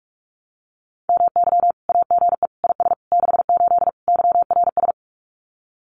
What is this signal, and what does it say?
Morse code, heard as a keyed tone, about 44 wpm, 700 Hz. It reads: MY AGE IS 68 YRS
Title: Common QSO Phrases Encore 45wpm